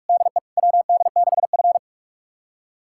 Morse code, heard as a keyed tone, about 44 wpm, 690 Hz. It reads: DE WD6F
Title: USA Callsign DE Callsign - 4 Characters - 45wpm